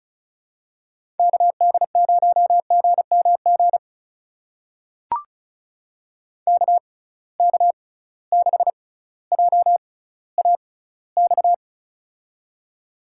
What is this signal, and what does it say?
Morse code, heard as a keyed tone, about 34 wpm, spaced as for 18 wpm, 700 Hz. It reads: KD0GMG KK6JAX
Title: USA Call Signs - 6 Characters - 6x Character Spacing - 35wpm